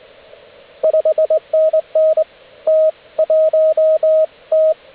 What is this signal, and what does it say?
Morse code, heard as a keyed tone, about 20 wpm, 630 Hz, with radio noise. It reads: 5NN T 1 T